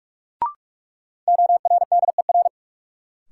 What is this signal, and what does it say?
Morse code, heard as a keyed tone, about 42 wpm, 700 Hz. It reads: ORDER
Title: Sets of 2 Words - New Word Drill - 45wpm